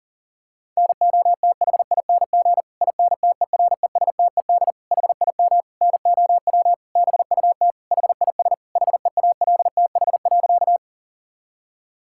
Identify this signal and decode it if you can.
Morse code, heard as a keyed tone, about 40 wpm, 700 Hz. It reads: NOTHING INTERESTED HIM NOW BUT HIS HEALTH.
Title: Sentences from Top 1000 Words - Review Part 6 - 40wpm